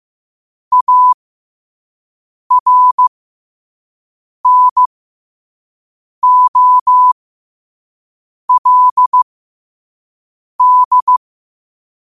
Morse code, heard as a keyed tone, about 15 wpm, spaced as for 5 wpm, 1000 Hz. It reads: ARNOLD
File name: qotdfiles / 5wpm